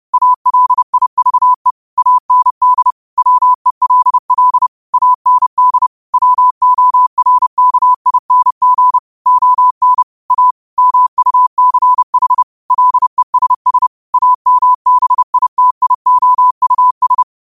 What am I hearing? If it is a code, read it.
Morse code, heard as a keyed tone, about 30 wpm, 1000 Hz. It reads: ALIVE AND WELL AND WORKING ON A MUCH LESS AMBITIOUS